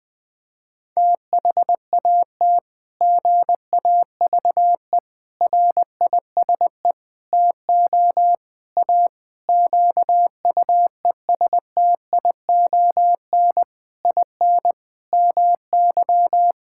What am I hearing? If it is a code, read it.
Morse code, heard as a keyed tone, about 21 wpm, 700 Hz. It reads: THAT GAVE RISE TO A QUESTION IN MY
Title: Sentences from Top 1000 Words - Review Part 6 - 20wpm